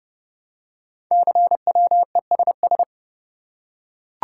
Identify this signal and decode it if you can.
Morse code, heard as a keyed tone, about 28 wpm, 700 Hz. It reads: CWESS
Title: Code Groups - 5 Letters - 30wpm